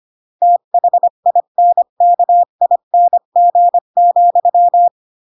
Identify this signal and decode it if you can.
Morse code, heard as a keyed tone, about 25 wpm, 700 Hz. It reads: THINKING,
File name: CW / jodorowsky / 82 commandments 25-20